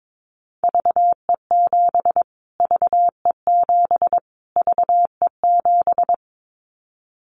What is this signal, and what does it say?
Morse code, heard as a keyed tone, about 22 wpm, 700 Hz. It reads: VE7 VE7 VE7
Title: Callsign Prefixes for Top 25 DXCC - Repeated 3x - 22wpm